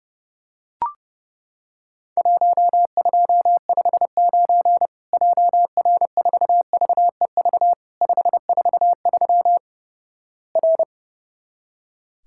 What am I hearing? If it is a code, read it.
Morse code, heard as a keyed tone, about 31 wpm, 700 Hz. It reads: 1259 JR4VEV 543 R
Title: Worked All Europe DX Contest - QTC Exchange - 30wpm